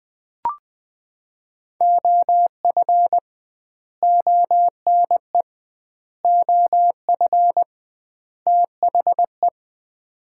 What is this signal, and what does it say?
Morse code, heard as a keyed tone, about 21 wpm, 700 Hz. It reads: OF ONE OF THE
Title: 2x Word-Spacing - Sets of 4 Words - 20wpm